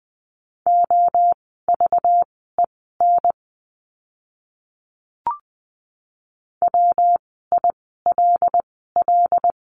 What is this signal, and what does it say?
Morse code, heard as a keyed tone, about 20 wpm, spaced as for 14 wpm, 700 Hz. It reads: OVEN WILL
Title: 192 Lesson 31f - Letter v - Words 20wpm